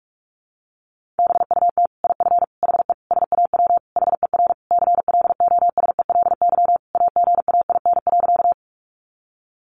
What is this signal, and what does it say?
Morse code, heard as a keyed tone, about 44 wpm, 700 Hz. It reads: BUT IF HE SAW HER CLOSELY AGAIN.
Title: Sentences from Top 1400 Words 45wpm